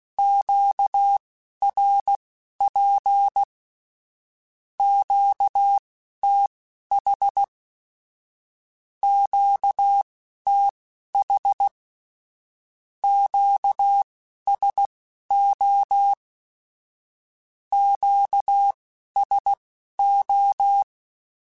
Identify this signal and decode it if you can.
Morse code, heard as a keyed tone, about 16 wpm, spaced as for 11 wpm, 800 Hz. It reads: QRP QTH QTH QSO QSO
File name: (spaced) / Q-Calls Spaced-mp3s / Q Calls sp Once @15